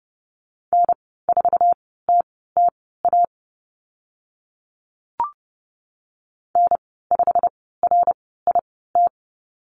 Morse code, heard as a keyed tone, about 31 wpm, spaced as for 17 wpm, 690 Hz. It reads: N4TTA N5RIT
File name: Lesson 15g - Number 5 - Call Signs 30wpm